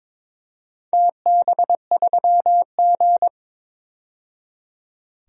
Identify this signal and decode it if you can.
Morse code, heard as a keyed tone, about 22 wpm, 700 Hz. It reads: TB3G
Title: DXCC-100 Call Signs - 4 Characters - 22wpm